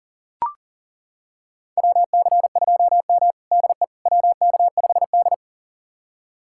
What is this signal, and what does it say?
Morse code, heard as a keyed tone, about 40 wpm, 700 Hz. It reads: WC2M DE WK5D
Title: USA Callsign DE Callsign - 4 Characters - 40wpm